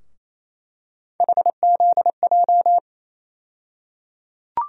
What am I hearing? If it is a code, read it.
Morse code, heard as a keyed tone, about 28 wpm, 700 Hz. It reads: HZJ
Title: USA Call Sign Postfixes - 3 Letters 28wpm